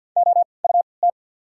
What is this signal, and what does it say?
Morse code, heard as a keyed tone, about 48 wpm, spaced as for 28 wpm, 700 Hz. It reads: OUT
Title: 3x Character Spacing - Top 100 Words - 50wpm